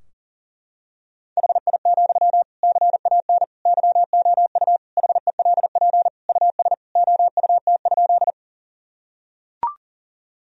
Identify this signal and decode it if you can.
Morse code, heard as a keyed tone, about 40 wpm, 700 Hz. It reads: HI, CAN YOU HELP US OUT?
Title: Sentences from Top 1000 Words - Review Part 3 - 40wpm